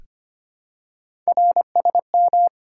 Morse code, heard as a keyed tone, about 25 wpm, 700 Hz. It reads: RSM
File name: Three Letter-Number Combinations - 25wpm